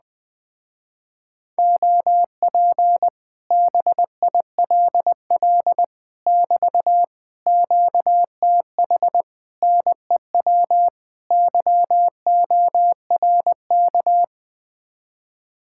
Morse code, heard as a keyed tone, about 21 wpm, 700 Hz. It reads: OP BILL <BT> QTH NEW YORK